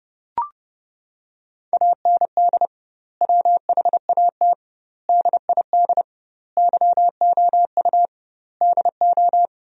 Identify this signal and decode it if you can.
Morse code, heard as a keyed tone, about 30 wpm, 700 Hz. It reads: AND WHAT DID YOU DO
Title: 2x Word-Spacing - Sentences from Top 100 Words - 30wpm